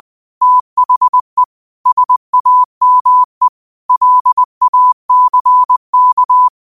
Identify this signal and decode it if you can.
Morse code, heard as a keyed tone, about 20 wpm, 1000 Hz. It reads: THE SAME LACK
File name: qotdfiles / 20wpm